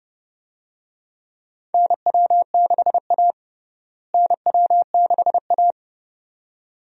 Morse code, heard as a keyed tone, about 29 wpm, 700 Hz. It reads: NW6A NW6A
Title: USA Call Signs - 4 Characters - Repeated 2x - 30wpm